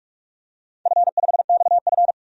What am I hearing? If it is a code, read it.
Morse code, heard as a keyed tone, about 46 wpm, 700 Hz. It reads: F5XF
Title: DXCC-100 Call Signs - 4 Characters - Repeated 3x - 45wpm